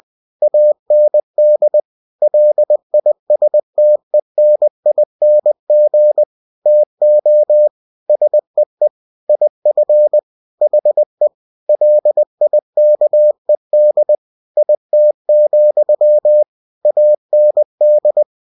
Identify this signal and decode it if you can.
Morse code, heard as a keyed tone, about 20 wpm, 600 Hz. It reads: AND LISTENING TO SEE IF HE LIKED IT, AND
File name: cw-books / winnie-the-pooh / winnie-20wpm-page-0101